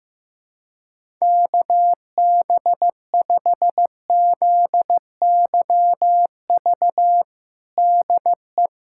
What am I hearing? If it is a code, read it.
Morse code, heard as a keyed tone, about 15 wpm, 700 Hz. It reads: KB5ZYV DE
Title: USA Callsign DE Callsign - 6 Characters - 15wpm